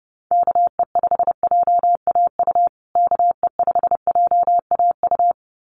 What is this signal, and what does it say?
Morse code, heard as a keyed tone, about 30 wpm, 700 Hz. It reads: KE5JAU KE5JAU
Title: USA Call Signs Repeated 2x 30wpm